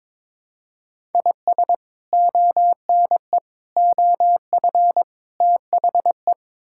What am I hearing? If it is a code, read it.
Morse code, heard as a keyed tone, about 22 wpm, 700 Hz. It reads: IS ONE OF THE
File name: practice / 22wpm / Build A Phrase - 5-Word Phrases - 22wpm